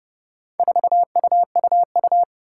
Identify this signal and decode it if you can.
Morse code, heard as a keyed tone, about 30 wpm, 700 Hz. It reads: 4UUU
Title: Four Challenging Letter-Number Combinations 30wpm